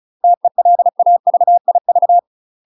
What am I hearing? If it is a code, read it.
Morse code, heard as a keyed tone, about 37 wpm, 700 Hz. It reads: TELAVIV
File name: UK MASTER QSOs TRAINING / 35WPM / UK MASTER QSOs 0021